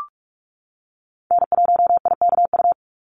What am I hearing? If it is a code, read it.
Morse code, heard as a keyed tone, about 44 wpm, 700 Hz. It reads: N1IKU
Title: USA Call Signs 45wpm